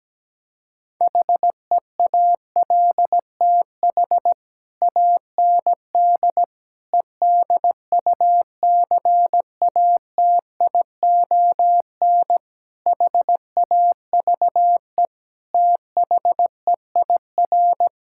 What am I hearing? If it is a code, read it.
Morse code, heard as a keyed tone, about 17 wpm, 700 Hz. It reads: HEALTH AND EDUCATION HAVE THEIR
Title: Sentences from Top 1000 Words - Review Part 6 - 17wpm